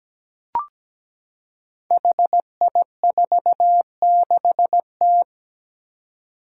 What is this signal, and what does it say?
Morse code, heard as a keyed tone, about 17 wpm, 700 Hz. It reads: HI46T
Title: Five Challenging Letter-Number Combinations 17wpm